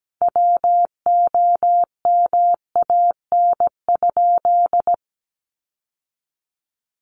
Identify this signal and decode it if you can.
Morse code, heard as a keyed tone, about 17 wpm, 700 Hz. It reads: WOMAN?
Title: Sentences from Top 1100 Words 17wpm